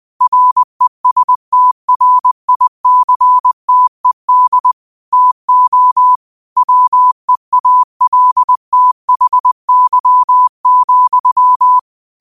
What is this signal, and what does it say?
Morse code, heard as a keyed tone, about 20 wpm, 1000 Hz. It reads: RESTRICTED TO WEALTHY,